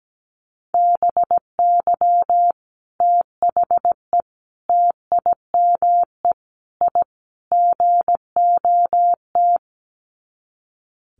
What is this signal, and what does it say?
Morse code, heard as a keyed tone, about 17 wpm, 700 Hz. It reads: BY THE TIME I GOT
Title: Sets of 5 Words Encore2 17wpm